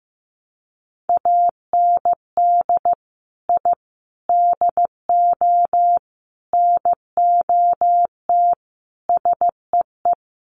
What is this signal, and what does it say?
Morse code, heard as a keyed tone, about 15 wpm, 700 Hz. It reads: AND I DO NOT SEE
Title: Sets of 5 Words Encore2 15wpm